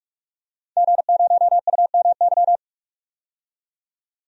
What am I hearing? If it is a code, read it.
Morse code, heard as a keyed tone, about 46 wpm, 700 Hz. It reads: G0UMY
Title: DXCC-100 Call Signs - 5 Characters - 45wpm